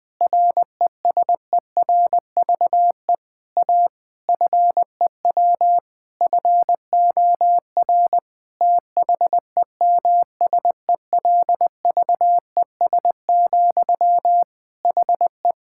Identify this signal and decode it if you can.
Morse code, heard as a keyed tone, about 21 wpm, 700 Hz. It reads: RESERVE A FEW FOR THEMSELVES, HE
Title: Sentences from Top 1000 Words - Review Part 6 - 20wpm